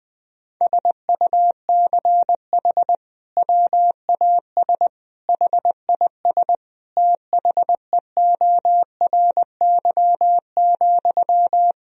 Morse code, heard as a keyed tone, about 20 wpm, 700 Hz. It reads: SUCH WAS HIS THEORY,
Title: Sentences from Top 1000 Words - Review Part 4 - 20wpm